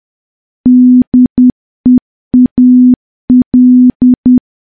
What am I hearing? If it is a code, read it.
Morse code, heard as a keyed tone, about 10 wpm, 250 Hz. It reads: DEAL
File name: morse / morse-training-10WPM